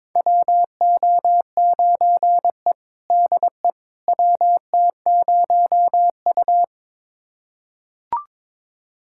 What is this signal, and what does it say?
Morse code, heard as a keyed tone, about 22 wpm, 700 Hz. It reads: WO9E DE WT0U
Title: USA Callsign DE Callsign - 4 Characters - 22wpm